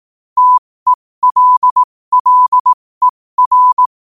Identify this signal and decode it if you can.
Morse code, heard as a keyed tone, about 18 wpm, spaced as for 15 wpm, 1000 Hz. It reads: TELLER